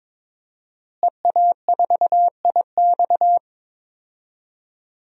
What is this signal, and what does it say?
Morse code, heard as a keyed tone, about 22 wpm, 700 Hz. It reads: EA4IX
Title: DXCC-100 Call Signs - 5 Characters - Repeated 3x - 22wpm